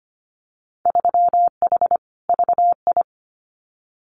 Morse code, heard as a keyed tone, about 25 wpm, 700 Hz. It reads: 3H VI
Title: Winter Field Day Exchanges Encore 25wpm